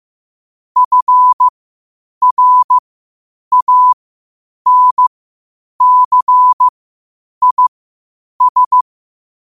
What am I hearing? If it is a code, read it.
Morse code, heard as a keyed tone, about 15 wpm, spaced as for 8 wpm, 1000 Hz. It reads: FRANCIS
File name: qotdfiles / 8wpm